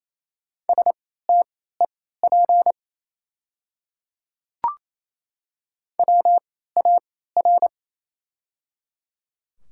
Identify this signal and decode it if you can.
Morse code, heard as a keyed tone, about 28 wpm, spaced as for 16 wpm, 700 Hz. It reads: STEP WAR
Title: 3x Character Spacing - Top 500 Words - 28wpm